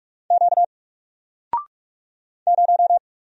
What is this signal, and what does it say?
Morse code, heard as a keyed tone, about 46 wpm, 700 Hz. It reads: Q 0
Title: Single Letter-Number Alt 45wpm